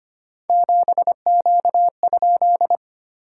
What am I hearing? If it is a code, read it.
Morse code, heard as a keyed tone, about 25 wpm, 700 Hz. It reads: 7Q?
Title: SOTA QSO Practice 25wpm